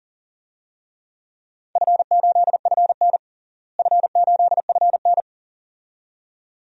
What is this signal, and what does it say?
Morse code, heard as a keyed tone, about 37 wpm, 700 Hz. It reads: F8FN F8FN
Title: DXCC-100 Call Signs - 4 Characters - Repeated 2x - 40wpm